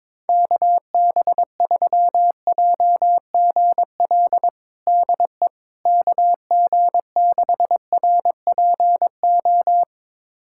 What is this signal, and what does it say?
Morse code, heard as a keyed tone, about 22 wpm, 700 Hz. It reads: KB3JGL DE KG6RPO
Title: USA Callsign DE Callsign 22wpm